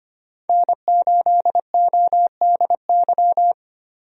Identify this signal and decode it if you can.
Morse code, heard as a keyed tone, about 25 wpm, 700 Hz. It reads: N8ODY
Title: USA Call Signs - 5 Characters - 25wpm